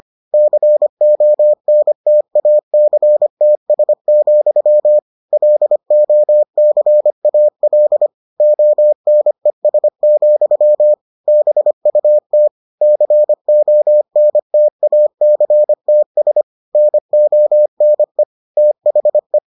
Morse code, heard as a keyed tone, about 25 wpm, 600 Hz. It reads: CONTACTS, LOCAL ONES, BUT CONTACTS NONE THE